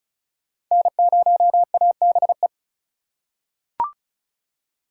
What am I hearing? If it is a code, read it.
Morse code, heard as a keyed tone, about 34 wpm, 690 Hz. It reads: N0ABE
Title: USA Call Signs - 5 Characters - 3x Character Spacing - 35wpm